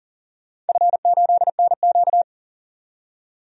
Morse code, heard as a keyed tone, about 40 wpm, 700 Hz. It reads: F8NQ
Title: DXCC-100 Call Signs - 4 Characters - 40wpm